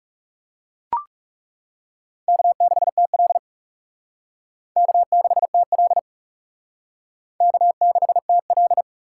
Morse code, heard as a keyed tone, about 37 wpm, 700 Hz. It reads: K6TL K6TL K6TL
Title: Speed Racing - USA 1x2 Call Signs 30wpm